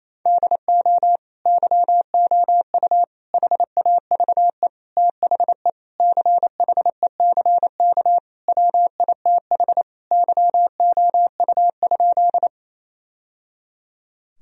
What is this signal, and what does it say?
Morse code, heard as a keyed tone, about 28 wpm, 700 Hz. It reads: DO YOU HAVE THE CHECK WITH YOU?
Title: Sentences from Top 1000 Words - Review Part 3 - 28wpm